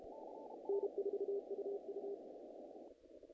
Morse code, heard as a keyed tone, about 32 wpm, 400 Hz. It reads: N4UU
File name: contests / naqp-cw-k5tr-op-k5pi- -1800